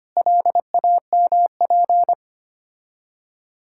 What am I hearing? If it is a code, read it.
Morse code, heard as a keyed tone, about 25 wpm, 700 Hz. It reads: LAMP
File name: practice / 25wpm / Difficult Words with Hints - 25wpm